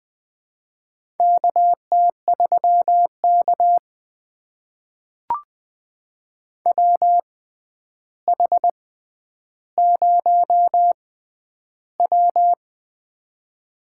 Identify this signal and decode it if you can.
Morse code, heard as a keyed tone, about 21 wpm, spaced as for 10 wpm, 700 Hz. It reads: KT3K WH0W
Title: USA Call Signs - 4 Characters - 6x Character Spacing - 20wpm